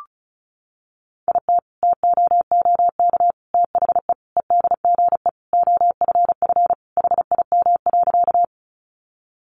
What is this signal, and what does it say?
Morse code, heard as a keyed tone, about 34 wpm, 700 Hz. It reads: IT TOOK THE EDGE OFF HIM.
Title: Sentences from Top 1300 Words 35wpm